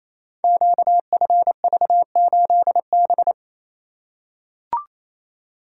Morse code, heard as a keyed tone, about 27 wpm, 700 Hz. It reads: QFV8B
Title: Code Groups - 5 Letter:Numbers - 28wpm